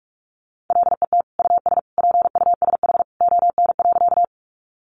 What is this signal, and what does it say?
Morse code, heard as a keyed tone, about 44 wpm, 700 Hz. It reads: LET US PUSH ON.
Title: Sentences from Top 1300 Words 45wpm